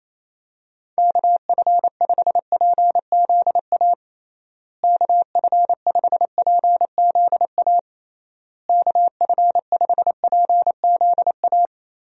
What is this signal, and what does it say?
Morse code, heard as a keyed tone, about 28 wpm, 700 Hz. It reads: KF5PZA KF5PZA KF5PZA
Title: USA Call Signs - 6 Characters - Repeated 3x - 28wpm